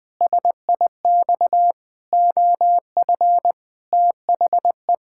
Morse code, heard as a keyed tone, about 20 wpm, 700 Hz. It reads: SIX OF THE
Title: Pangram Challenge v2 - 20wpm